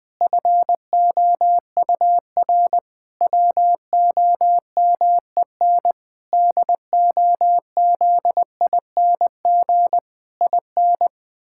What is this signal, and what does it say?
Morse code, heard as a keyed tone, about 20 wpm, 700 Hz. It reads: FOUR WOMEN DOZING IN
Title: Pangram Challenge v2 - 20wpm